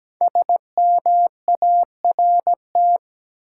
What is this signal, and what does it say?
Morse code, heard as a keyed tone, about 17 wpm, 700 Hz. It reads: SMART
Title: Five Letter Words - 17wpm